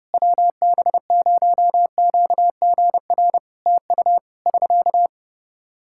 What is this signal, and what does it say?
Morse code, heard as a keyed tone, about 29 wpm, 700 Hz. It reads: WB0QGR TU <SK>
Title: Practice QSO - Exchange 4 - Ending - 30wpm